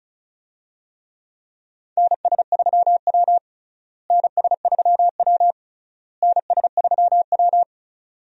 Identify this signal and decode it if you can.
Morse code, heard as a keyed tone, about 36 wpm, 700 Hz. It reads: NS3W NS3W NS3W
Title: USA Call Signs - 4 Characters - Repeated 3x - 35wpm